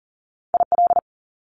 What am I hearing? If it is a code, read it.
Morse code, heard as a keyed tone, about 39 wpm, 700 Hz. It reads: IL
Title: Top Two Letter Combinations in Words 40wpm